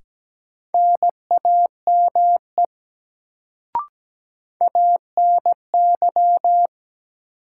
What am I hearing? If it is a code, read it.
Morse code, heard as a keyed tone, about 17 wpm, 700 Hz. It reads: NAME ANY
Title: Sentences from Top 200 Words - New Word Drill - 17wpm